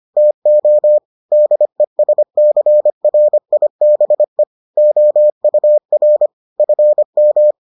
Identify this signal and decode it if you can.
Morse code, heard as a keyed tone, about 25 wpm, 600 Hz. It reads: TO DESCRIBE OUR FM